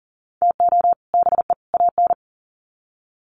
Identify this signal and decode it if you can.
Morse code, heard as a keyed tone, about 40 wpm, 700 Hz. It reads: TO BE AN
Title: Sets of 3 Words 40wpm